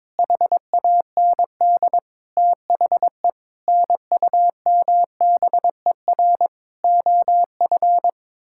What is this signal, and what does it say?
Morse code, heard as a keyed tone, about 21 wpm, 700 Hz. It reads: HAND THE NUMBER OF
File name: Sets of 9 Words - 22wpm